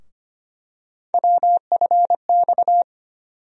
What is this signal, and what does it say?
Morse code, heard as a keyed tone, about 25 wpm, 700 Hz. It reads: WFX
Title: USA Call Sign Postfixes - 3 Letters 25wpm